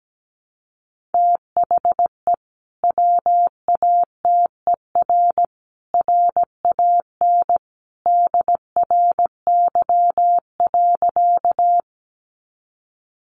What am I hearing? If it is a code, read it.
Morse code, heard as a keyed tone, about 17 wpm, 700 Hz. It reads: THE WATER RAN DRY.
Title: Sentences from Top 500 Words 17wpm